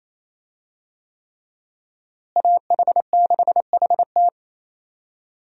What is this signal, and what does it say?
Morse code, heard as a keyed tone, about 28 wpm, 700 Hz. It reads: AH6HT
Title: USA Call Signs - 5 Characters - Repeated 3x - 28wpm